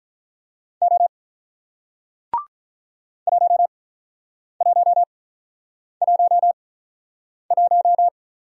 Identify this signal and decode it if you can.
Morse code, heard as a keyed tone, about 48 wpm, spaced as for 9 wpm, 700 Hz. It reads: O 1111